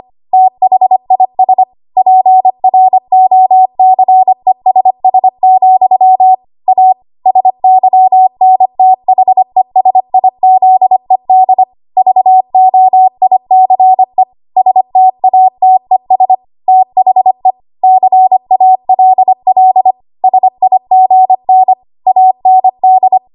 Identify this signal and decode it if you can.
Morse code, heard as a keyed tone, about 25 wpm, 750 Hz. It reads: THIS PROCESS, A SYNTHESIZED VOICE STATES THE CALL SIGN AND
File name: Morse / 25WPM